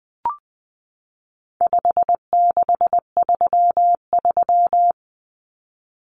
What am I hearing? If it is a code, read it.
Morse code, heard as a keyed tone, about 20 wpm, 700 Hz. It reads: 5633
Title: Numbers - 4-Digit 20wpm